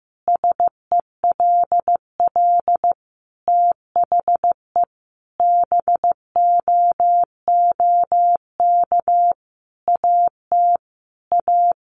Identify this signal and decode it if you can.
Morse code, heard as a keyed tone, about 14 wpm, 700 Hz. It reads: SELL THE BOOK AT A
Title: Sentences from Top 1200 Words 15wpm